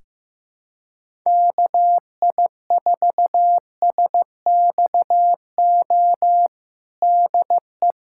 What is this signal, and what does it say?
Morse code, heard as a keyed tone, about 15 wpm, 700 Hz. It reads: KI4SXO DE
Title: USA Callsign DE Callsign 15wpm